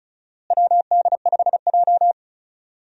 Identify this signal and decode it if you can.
Morse code, heard as a keyed tone, about 34 wpm, 700 Hz. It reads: WD5J
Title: USA Call Signs - 4 Characters - Repeated 2x - 35wpm